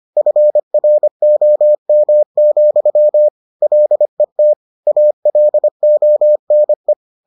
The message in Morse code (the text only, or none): FROM, LET ALONE